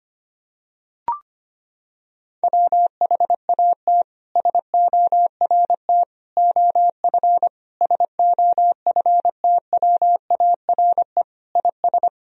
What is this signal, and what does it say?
Morse code, heard as a keyed tone, about 25 wpm, 700 Hz. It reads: WHAT SORT OF SOFTWARE IS
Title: Sentences from Top 1000 Words - Review Part 3 - 25wpm